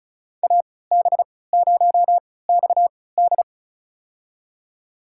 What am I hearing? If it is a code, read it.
Morse code, heard as a keyed tone, about 34 wpm, spaced as for 20 wpm, 700 Hz. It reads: AB0XD
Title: USA Call Signs - 5 Characters - 3x Character Spacing - 35wpm